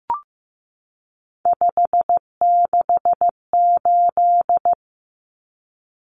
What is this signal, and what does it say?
Morse code, heard as a keyed tone, about 15 wpm, 700 Hz. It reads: 568
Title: Numbers - 3-Digit 15wpm